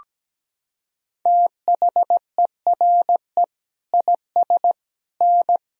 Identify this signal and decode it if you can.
Morse code, heard as a keyed tone, about 17 wpm, 700 Hz. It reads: THERE IS N
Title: Sentences from Top 200 Words and Abbreviations 17wpm